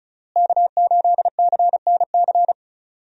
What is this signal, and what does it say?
Morse code, heard as a keyed tone, about 34 wpm, 700 Hz. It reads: K8CNC
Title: USA Call Signs - 5 Characters - Repeated 3x - 35wpm